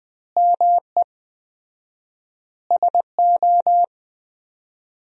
Morse code, heard as a keyed tone, about 21 wpm, 700 Hz. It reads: ME SO
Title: 4x Word-Spacing - Sentences from Top 400 Words - 20wpm